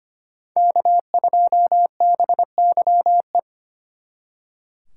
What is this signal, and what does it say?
Morse code, heard as a keyed tone, about 31 wpm, 700 Hz. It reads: K2BYE
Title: Speed Racing - USA 1x3 Call Signs 25wpm